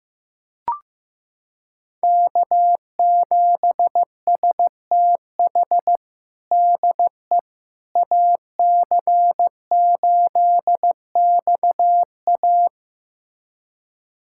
Morse code, heard as a keyed tone, about 16 wpm, 700 Hz. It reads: K7STH DE AC8XA
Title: USA Callsign DE Callsign - 5 Characters - 15wpm